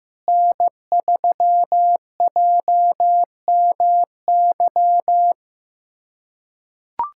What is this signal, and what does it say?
Morse code, heard as a keyed tone, about 15 wpm, 700 Hz. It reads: N3JMY
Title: USA Callsign DE Callsign 15wpm